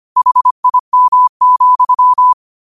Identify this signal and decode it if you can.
Morse code, heard as a keyed tone, about 25 wpm, 1000 Hz. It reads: HIM,